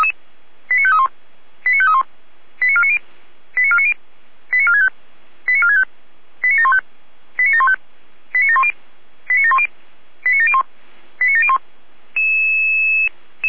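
Five-tone selective call (ZVEI), heard as quick runs of five tones, tones of about 70 ms. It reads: 89531 89531 89490 89490 89465 89465 89715 89715 89710 89710 89701 89701
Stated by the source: ZVEI five-tone selective call received by SDR.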